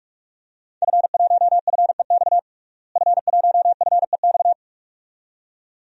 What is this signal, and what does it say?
Morse code, heard as a keyed tone, about 44 wpm, 700 Hz. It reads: F1FEX F1FEX
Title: DXCC-100 Call Signs - 5 Characters - Repeated 2x - 45wpm